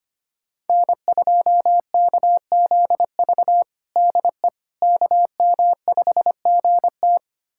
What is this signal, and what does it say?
Morse code, heard as a keyed tone, about 25 wpm, 700 Hz. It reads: N2KZV DE KM5GT
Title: USA Callsign DE Callsign - 5 Characters - 25wpm